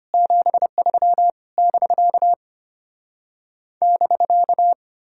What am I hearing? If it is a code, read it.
Morse code, heard as a keyed tone, about 33 wpm, 700 Hz. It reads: 73 <BK> <BK>